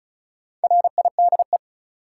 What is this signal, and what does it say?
Morse code, heard as a keyed tone, about 36 wpm, 700 Hz. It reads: RIDE
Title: Sentences from Top 1000 Words - Review Part 6 - New Word Drill - 35wpm